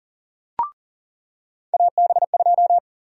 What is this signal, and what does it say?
Morse code, heard as a keyed tone, about 38 wpm, 700 Hz. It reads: AB2
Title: USA Call Sign Prefixes - 40wpm